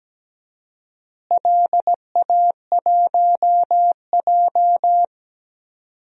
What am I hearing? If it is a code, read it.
Morse code, heard as a keyed tone, about 17 wpm, 700 Hz. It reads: LA1J
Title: DXCC-100 Call Signs - 4 Characters - Repeated 3x - 17wpm